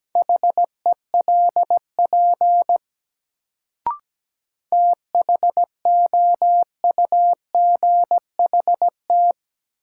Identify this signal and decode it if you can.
Morse code, heard as a keyed tone, about 17 wpm, 700 Hz. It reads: HELP THOUGHT
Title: Sets of 2 Words - New Word Drill - 17wpm